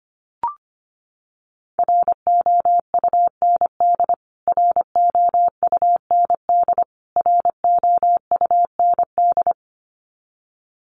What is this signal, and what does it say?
Morse code, heard as a keyed tone, about 25 wpm, 700 Hz. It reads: ROUND ROUND ROUND
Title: Top 200 Words Repeated 25wpm